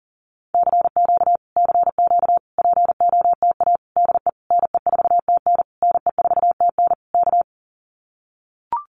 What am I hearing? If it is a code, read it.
Morse code, heard as a keyed tone, about 40 wpm, 700 Hz. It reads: CQ CQ POTA DE NE4TN NE4TN K
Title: Common QSO Phrases Encore 40wpm